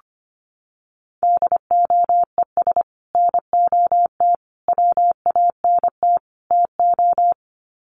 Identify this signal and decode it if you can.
Morse code, heard as a keyed tone, about 25 wpm, 700 Hz. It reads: DOES NOT WANT TO
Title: Sets of 4 Words 25wpm